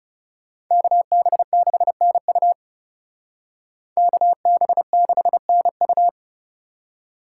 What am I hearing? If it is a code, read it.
Morse code, heard as a keyed tone, about 40 wpm, 700 Hz. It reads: KB6NU KB6NU